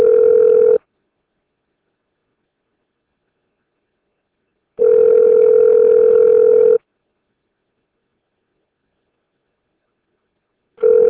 Ringback tone (North America), 440 + 480 Hz.